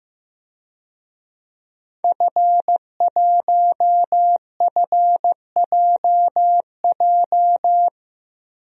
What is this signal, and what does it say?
Morse code, heard as a keyed tone, about 15 wpm, 700 Hz. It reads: F1FJJ
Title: DXCC-100 Call Signs - 5 Characters - 15wpm